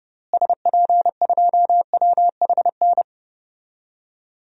Morse code, heard as a keyed tone, about 29 wpm, 700 Hz. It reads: SP2WHN
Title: DXCC-100 Call Signs - 6 Characters - Repeated 2x - 30wpm